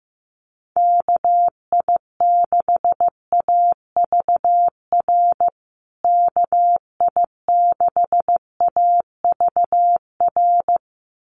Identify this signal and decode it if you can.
Morse code, heard as a keyed tone, about 15 wpm, 700 Hz. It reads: KI6AVR KI6AVR
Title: USA Call Signs Repeated 2x 15wpm